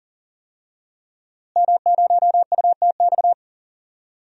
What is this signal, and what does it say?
Morse code, heard as a keyed tone, about 40 wpm, 700 Hz. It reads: M0UTX
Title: DXCC-100 Call Signs - 5 Characters - Repeated 2x - 40wpm